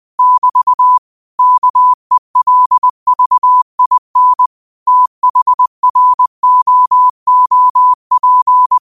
Morse code, heard as a keyed tone, about 20 wpm, 1000 Hz. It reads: <BT> KELVIN THROOP
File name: qotdfiles / 20wpm